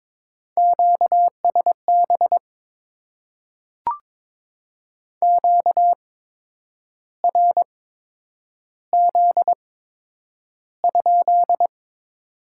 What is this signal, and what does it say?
Morse code, heard as a keyed tone, about 22 wpm, spaced as for 9 wpm, 700 Hz. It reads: QSB QRZ?
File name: Top 90 QSO Elements - 8x Character Spacing - 22wpm